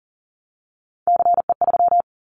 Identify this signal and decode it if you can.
Morse code, heard as a keyed tone, about 38 wpm, 690 Hz. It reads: CE3
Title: Callsign Prefixes for Top 100 DXCC - 40wpm